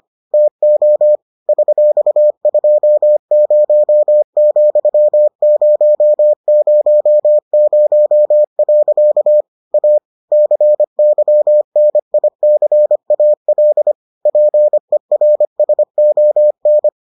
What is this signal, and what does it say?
Morse code, heard as a keyed tone, about 25 wpm, 600 Hz. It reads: TO $20,000. A CYNICAL PERSON